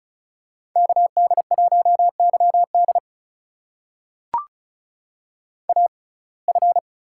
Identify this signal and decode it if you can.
Morse code, heard as a keyed tone, about 36 wpm, spaced as for 25 wpm, 700 Hz. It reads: KD1YD AF
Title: USA Call Signs - 5 Characters - 6x Character Spacing - 35wpm